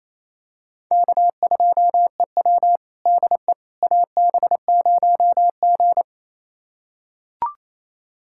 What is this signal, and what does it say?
Morse code, heard as a keyed tone, about 28 wpm, 700 Hz. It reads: K2EW DE AB0G
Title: USA Callsign DE Callsign - 4 Characters - 28wpm